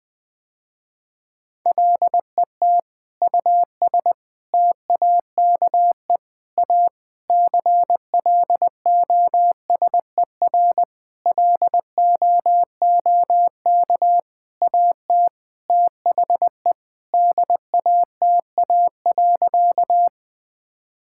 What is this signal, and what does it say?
Morse code, heard as a keyed tone, about 21 wpm, 700 Hz. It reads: LET US TAKE A CLOSER LOOK AT THE DATA.
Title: Sentences with 7-Word Phrases - Spoken Beginning - 20wpm